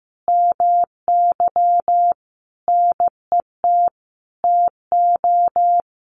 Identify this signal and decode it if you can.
Morse code, heard as a keyed tone, about 15 wpm, 700 Hz. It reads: MY NET TO
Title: Sentences from Top 1400 Words 15wpm